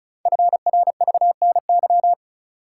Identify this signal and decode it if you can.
Morse code, heard as a keyed tone, about 36 wpm, 700 Hz. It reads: FRVNY
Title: Code Groups - 5 Letters - 35wpm